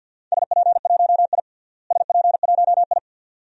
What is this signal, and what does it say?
Morse code, heard as a keyed tone, about 52 wpm, 700 Hz. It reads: SP1I SP1I
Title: DXCC-100 Call Signs - 4 Characters - Repeated 2x - 50wpm